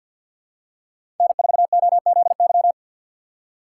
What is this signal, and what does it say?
Morse code, heard as a keyed tone, about 48 wpm, 700 Hz. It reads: N4OZY
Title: USA Call Signs - 5 Characters - 6x Character Spacing - 50wpm